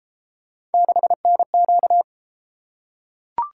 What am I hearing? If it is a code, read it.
Morse code, heard as a keyed tone, about 33 wpm, 700 Hz. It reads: 6NQ